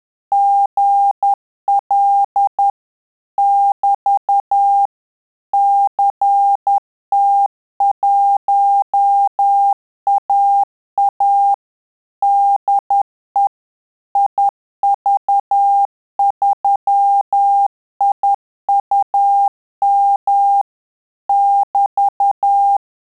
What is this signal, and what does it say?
Morse code, heard as a keyed tone, about 10 wpm, 800 Hz. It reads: GL <BT> CT1AA DE IV3IUM <BT>
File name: Telegraphy-group / QSO CW 10 PPM